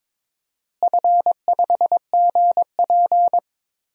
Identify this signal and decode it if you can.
Morse code, heard as a keyed tone, about 22 wpm, 700 Hz. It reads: F5GP
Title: DXCC-100 Call Signs - 4 Characters - Repeated 2x - 22wpm